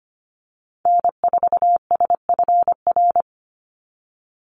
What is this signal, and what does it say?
Morse code, heard as a keyed tone, about 25 wpm, 700 Hz. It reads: N4SFR
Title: NA SOTA Activator Callsigns 25wpm